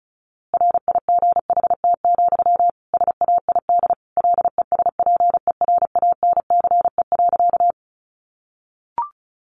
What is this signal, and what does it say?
Morse code, heard as a keyed tone, about 34 wpm, 700 Hz. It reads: RIGHT, SAID LESPERANCE.
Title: A Sound Of Thunder 35wpm